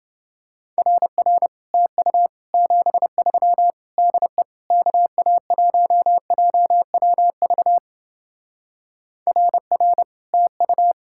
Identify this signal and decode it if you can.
Morse code, heard as a keyed tone, about 28 wpm, 700 Hz. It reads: RR TU 73 DE KA1JWV RR TU